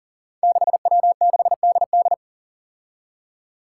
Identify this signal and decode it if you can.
Morse code, heard as a keyed tone, about 40 wpm, 700 Hz. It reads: 6W6DD
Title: Five Challenging Letter-Number Combinations 40wpm